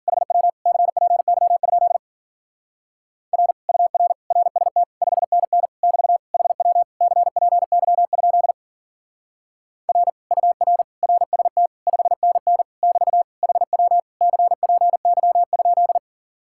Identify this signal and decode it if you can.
Morse code, heard as a keyed tone, about 47 wpm, 700 Hz. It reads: HW CPY? R UR RST 5NN <BT> HW CPY? R UR RST 5NN <BT> HW CPY?
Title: Speed-Racing Common QSO Phrases Encore - 40wpm